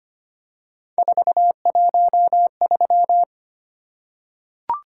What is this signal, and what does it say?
Morse code, heard as a keyed tone, about 25 wpm, 700 Hz. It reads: 413
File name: Three Letter-Number Combinations - 25wpm